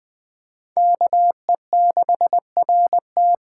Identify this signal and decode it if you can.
Morse code, heard as a keyed tone, about 21 wpm, 700 Hz. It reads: KE6RT
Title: USA Callsign DE Callsign 20wpm